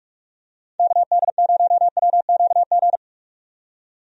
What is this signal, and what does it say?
Morse code, heard as a keyed tone, about 46 wpm, 700 Hz. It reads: KD0WQG
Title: USA Call Signs - 6 Characters - 45wpm